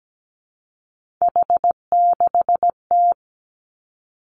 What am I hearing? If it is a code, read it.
Morse code, heard as a keyed tone, about 17 wpm, 700 Hz. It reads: H6T
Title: Three Challenging Letter-Number Combinations 17wpm